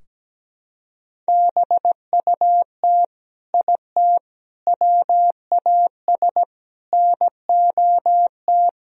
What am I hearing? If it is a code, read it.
Morse code, heard as a keyed tone, about 17 wpm, 700 Hz. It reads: BUT IT WAS NOT
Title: Sentences from Top 1000 Words - Review Part 4 - 17wpm